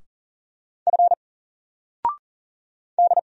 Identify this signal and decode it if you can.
Morse code, heard as a keyed tone, about 38 wpm, 700 Hz. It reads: F D
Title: Single Character (Letter:Number:Punctuation:Prosign) 40wpm